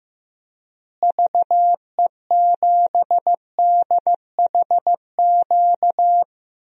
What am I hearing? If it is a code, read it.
Morse code, heard as a keyed tone, about 15 wpm, 700 Hz. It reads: VE7DHQ
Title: DXCC-100 Call Signs - 6 Characters - Repeated 3x - 15wpm